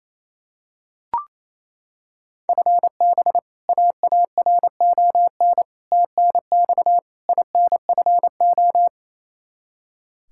Morse code, heard as a keyed tone, about 27 wpm, 700 Hz. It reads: FB AARON TNX INFO
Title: Practice QSO - Exchange 3 - More Info - 28wpm